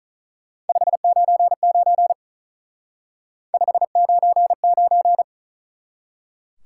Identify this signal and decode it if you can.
Morse code, heard as a keyed tone, about 43 wpm, 700 Hz. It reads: 599 599
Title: Speed-Racing Top 90 QSO Elements 35wpm